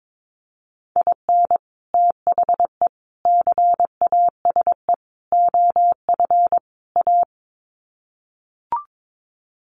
Morse code, heard as a keyed tone, about 22 wpm, 700 Hz. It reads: IN THE CASE OF A
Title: Sets of 5 Words Encore 22wpm